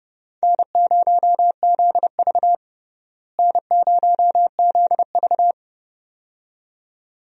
Morse code, heard as a keyed tone, about 29 wpm, 700 Hz. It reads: N0ZV N0ZV
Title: USA Call Signs - 4 Characters - Repeated 2x - 30wpm